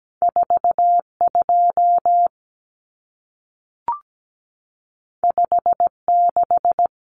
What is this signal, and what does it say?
Morse code, heard as a keyed tone, about 17 wpm, 700 Hz. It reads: 42 56
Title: Numbers - 2-Digit 17wpm